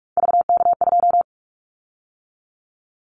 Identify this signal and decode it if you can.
Morse code, heard as a keyed tone, about 44 wpm, 700 Hz. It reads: VK2
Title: Callsign Prefixes for Top 100 DXCC - 45wpm